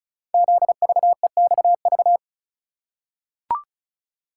Morse code, heard as a keyed tone, about 36 wpm, 690 Hz. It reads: ZVEXV
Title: Code Groups - 5 Letters - 35wpm